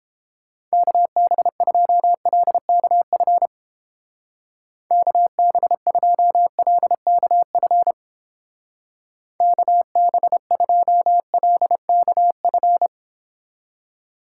Morse code, heard as a keyed tone, about 31 wpm, 700 Hz. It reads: KB2LKF KB2LKF KB2LKF